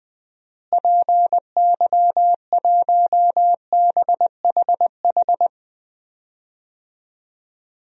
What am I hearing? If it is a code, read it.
Morse code, heard as a keyed tone, about 20 wpm, 700 Hz. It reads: PY1BHH
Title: DXCC-100 Call Signs - 6 Characters - Repeated 2x - 20wpm